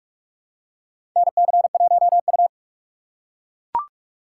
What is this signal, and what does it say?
Morse code, heard as a keyed tone, about 42 wpm, 700 Hz. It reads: NC1U
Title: USA Call Signs - 4 Characters - 6x Character Spacing - 45wpm